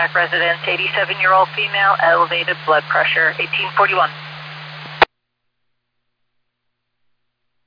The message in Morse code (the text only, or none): none